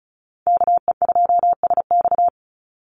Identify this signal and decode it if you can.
Morse code, heard as a keyed tone, about 34 wpm, 700 Hz. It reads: KE2SX
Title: Call Signs Repeated 2x - 35wpm